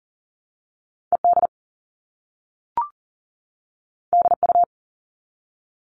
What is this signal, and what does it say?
Morse code, heard as a keyed tone, about 42 wpm, 700 Hz. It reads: ED DU
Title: All Cut Numbers - 2-Digit 40wpm